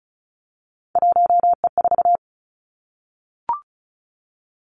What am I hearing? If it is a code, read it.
Morse code, heard as a keyed tone, about 36 wpm, 700 Hz. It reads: 1E4
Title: Three Challenging Letter-Number Combinations 35wpm